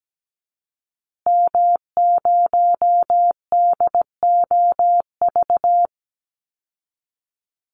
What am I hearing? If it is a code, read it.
Morse code, heard as a keyed tone, about 17 wpm, 700 Hz. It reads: M0DOV
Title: Call Signs Repeated 2x - 17wpm